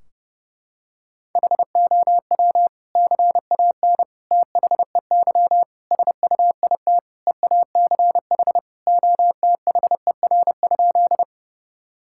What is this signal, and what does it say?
Morse code, heard as a keyed tone, about 32 wpm, 700 Hz. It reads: HOW CAN THEY SUIT EACH OTHER?
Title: Sentences from Top 1600 Words 30wpm